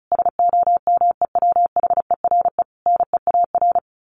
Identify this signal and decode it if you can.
Morse code, heard as a keyed tone, about 34 wpm, 700 Hz. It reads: SOMEWHERE NEAR
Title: Sentences from Top 1400 Words 35wpm